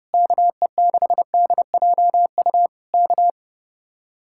Morse code, heard as a keyed tone, about 31 wpm, 700 Hz. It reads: KE6DJU K